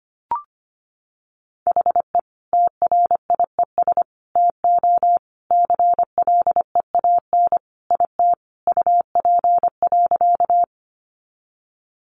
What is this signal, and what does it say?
Morse code, heard as a keyed tone, about 25 wpm, 700 Hz. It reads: HE TRIES TO CLEAN IT UP.
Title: Additional Easy Phrases 25wpm